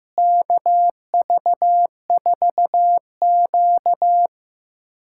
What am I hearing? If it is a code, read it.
Morse code, heard as a keyed tone, about 15 wpm, 700 Hz. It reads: KV4Q
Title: USA Callsign DE Callsign - 4 Characters - 15wpm